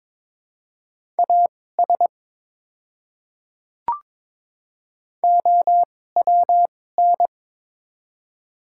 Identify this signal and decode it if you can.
Morse code, heard as a keyed tone, about 22 wpm, spaced as for 16 wpm, 700 Hz. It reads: AS OWN
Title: 2x Character Spacing - Top 300 Words - 22wpm